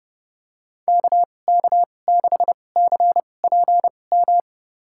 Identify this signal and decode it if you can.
Morse code, heard as a keyed tone, about 32 wpm, spaced as for 22 wpm, 700 Hz. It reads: KK6CPM
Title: USA Call Signs - 6 Characters - 2x Character Spacing - 30wpm